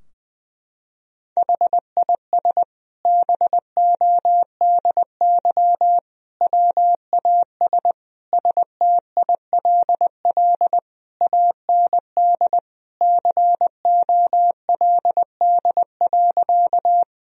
Morse code, heard as a keyed tone, about 20 wpm, 700 Hz. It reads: HIS BODY WAS STILL AND COLD.
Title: Sentences from Top 1000 Words - Review Part 2 - 20wpm